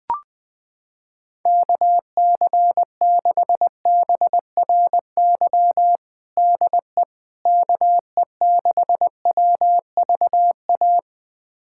Morse code, heard as a keyed tone, about 21 wpm, 700 Hz. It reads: KC6BRY DE KE6WVA
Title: USA Callsign DE Callsign - 6 Characters - 20wpm